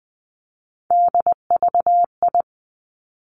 Morse code, heard as a keyed tone, about 20 wpm, 700 Hz. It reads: DVI
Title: Three Challenging Letter-Number Combinations 20wpm